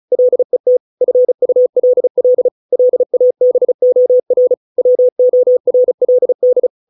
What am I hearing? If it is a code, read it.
Morse code, heard as a keyed tone, about 34 wpm, 500 Hz. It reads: LET FULL LABOR WORLD